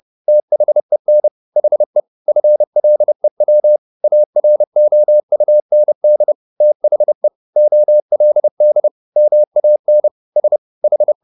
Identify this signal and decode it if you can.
Morse code, heard as a keyed tone, about 30 wpm, 600 Hz. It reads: THEN HE FLEW AROUND THE OLD MAN S H
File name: the old man and the sea / the old man and the sea 1 30wpm